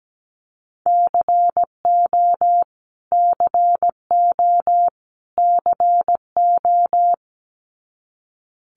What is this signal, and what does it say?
Morse code, heard as a keyed tone, about 17 wpm, 700 Hz. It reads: CO CO CO
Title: US States Repeated 3x 17wpm